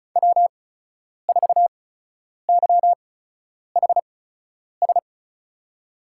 Morse code, heard as a keyed tone, about 36 wpm, spaced as for 10 wpm, 700 Hz. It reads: W4YHS
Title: USA Call Signs - 5 Characters - 8x Character Spacing - 35wpm